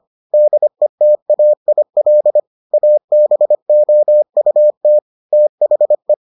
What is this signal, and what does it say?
Morse code, heard as a keyed tone, about 25 wpm, 600 Hz. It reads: DETAIL ABOUT THE